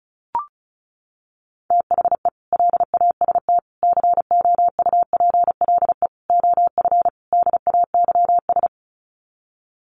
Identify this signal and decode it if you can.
Morse code, heard as a keyed tone, about 34 wpm, 700 Hz. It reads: THE LAST COUPLE OF DAYS
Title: Sets of 5 Words Encore 35wpm